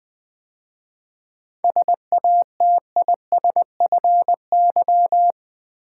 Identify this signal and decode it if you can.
Morse code, heard as a keyed tone, about 21 wpm, 700 Hz. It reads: SATISFY
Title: 7 Letter Words - First 3 Letters Given - 20wpm